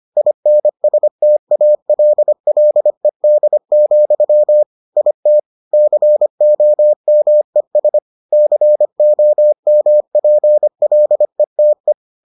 Morse code, heard as a keyed tone, about 25 wpm, 600 Hz. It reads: INSTALLED, IT COMES COMPLETE